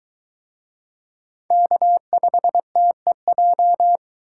Morse code, heard as a keyed tone, about 21 wpm, 700 Hz. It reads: K5TEJ